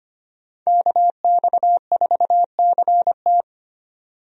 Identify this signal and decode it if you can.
Morse code, heard as a keyed tone, about 25 wpm, 700 Hz. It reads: KX4CT
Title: USA Call Signs - 5 Characters - 25wpm